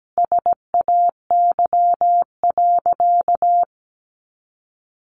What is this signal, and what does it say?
Morse code, heard as a keyed tone, about 17 wpm, 700 Hz. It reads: SAY.
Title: Sentences from Top 1200 Words 17wpm